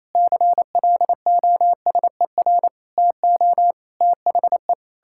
Morse code, heard as a keyed tone, about 28 wpm, 700 Hz. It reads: CLOSER TO THE
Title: Pangram Challenge v2 - 28wpm